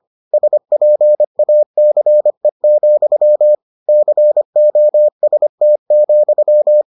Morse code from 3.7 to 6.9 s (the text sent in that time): COST,